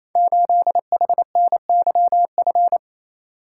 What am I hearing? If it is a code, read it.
Morse code, heard as a keyed tone, about 27 wpm, 700 Hz. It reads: 8HNYF
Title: Code Groups - 5 Letter:Numbers - 28wpm